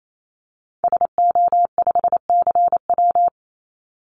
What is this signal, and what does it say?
Morse code, heard as a keyed tone, about 28 wpm, 700 Hz. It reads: SO5CW
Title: Call Signs - 28wpm